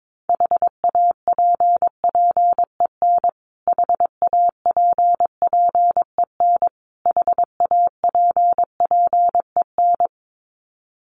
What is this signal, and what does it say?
Morse code, heard as a keyed tone, about 22 wpm, 700 Hz. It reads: HAPPEN HAPPEN HAPPEN
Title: Top 500 Words Repeated 22wpm